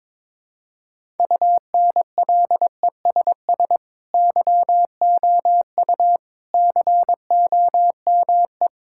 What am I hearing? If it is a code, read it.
Morse code, heard as a keyed tone, about 22 wpm, 700 Hz. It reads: UNLESS YOU COME
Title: Sentences from Top 1000 Words - Review Part 6 - 22wpm